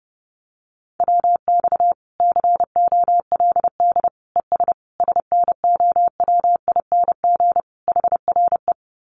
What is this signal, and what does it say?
Morse code, heard as a keyed tone, about 30 wpm, 700 Hz. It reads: WX COLD ES SNOWING HRE
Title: Common QSO Phrases 30wpm